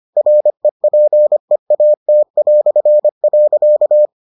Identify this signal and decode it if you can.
Morse code, heard as a keyed tone, about 25 wpm, 600 Hz. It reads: REPEAT".